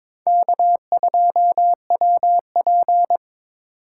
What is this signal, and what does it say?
Morse code, heard as a keyed tone, about 22 wpm, 700 Hz. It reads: K2WP
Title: USA Call Signs - 4 Characters - 2x Character Spacing - 22wpm